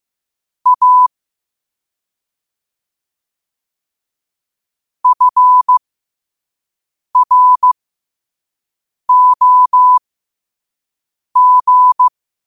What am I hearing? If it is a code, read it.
Morse code, heard as a keyed tone, about 15 wpm, spaced as for 5 wpm, 1000 Hz. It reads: A FROG
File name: qotdfiles / 5wpm